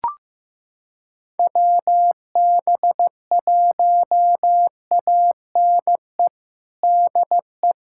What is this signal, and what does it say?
Morse code, heard as a keyed tone, about 15 wpm, 700 Hz. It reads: WB1ANE DE
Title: USA Callsign DE Callsign - 6 Characters - 15wpm